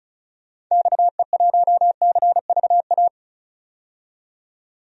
Morse code, heard as a keyed tone, about 36 wpm, 700 Hz. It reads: XE1CVA
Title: DXCC-100 Call Signs - 6 Characters - 35wpm